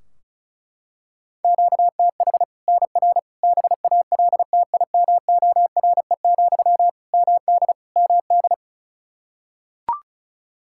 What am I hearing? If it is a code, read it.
Morse code, heard as a keyed tone, about 36 wpm, 700 Hz. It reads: QTH NR BALTIMORE, MD MD
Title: Practice QSO - Exchange 1 - Beginning - 35wpm